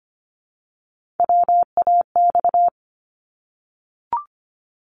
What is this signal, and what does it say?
Morse code, heard as a keyed tone, about 25 wpm, 700 Hz. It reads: WAX
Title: Multi-letter Phonemes Repeated 3x 25wpm